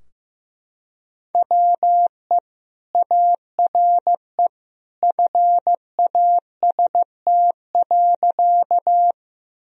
Morse code, heard as a keyed tone, about 15 wpm, 700 Hz. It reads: WE ARE FAST.
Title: Sentences from Top 1000 Words - Review Part 2 - 15wpm